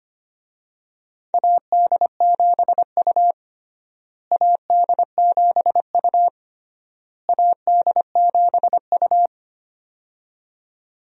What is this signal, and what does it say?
Morse code, heard as a keyed tone, about 25 wpm, 700 Hz. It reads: AD7U AD7U AD7U
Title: USA Call Signs - 4 Characters - Repeated 3x - 25wpm